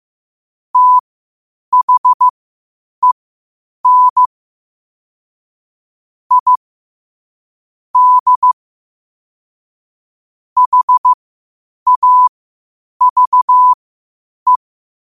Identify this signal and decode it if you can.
Morse code, heard as a keyed tone, about 15 wpm, spaced as for 8 wpm, 1000 Hz. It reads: THEN I D HAVE
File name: qotdfiles / 8wpm